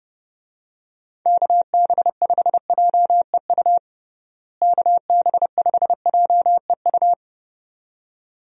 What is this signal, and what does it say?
Morse code, heard as a keyed tone, about 30 wpm, 700 Hz. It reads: KB5JEU KB5JEU
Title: USA Call Signs - 6 Characters - Repeated 2x - 30wpm